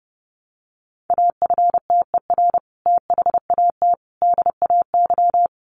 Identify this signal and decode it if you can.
Morse code, heard as a keyed tone, about 30 wpm, 700 Hz. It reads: AFTER THAT DAY
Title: Sentences from Top 1100 Words 30wpm